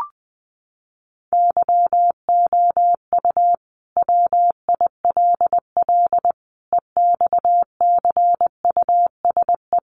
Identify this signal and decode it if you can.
Morse code, heard as a keyed tone, about 20 wpm, 700 Hz. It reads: YOU WILL EXCUSE
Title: Sentences from Top 700 Words 20wpm